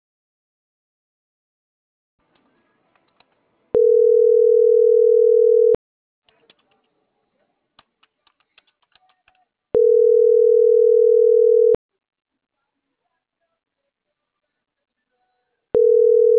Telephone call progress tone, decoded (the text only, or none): ringback tone